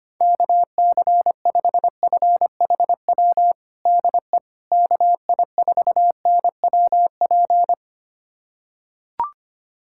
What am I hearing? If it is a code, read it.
Morse code, heard as a keyed tone, about 25 wpm, 700 Hz. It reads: KC5FHW DE KI4NWP
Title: USA Callsign DE Callsign 25wpm